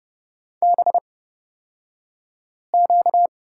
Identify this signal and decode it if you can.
Morse code, heard as a keyed tone, about 30 wpm, spaced as for 5 wpm, 700 Hz. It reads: BQ
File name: Lesson 38d - Letter q - Two Characters 30wpm